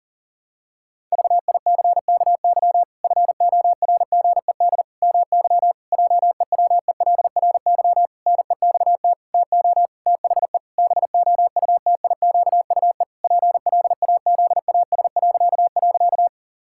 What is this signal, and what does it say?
Morse code, heard as a keyed tone, about 40 wpm, 700 Hz. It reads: VICKY FORGED MY JEWELRY NEXT TO THE BOUTIQUE PLAZAS..
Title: Pangram Challenge v2 - 40wpm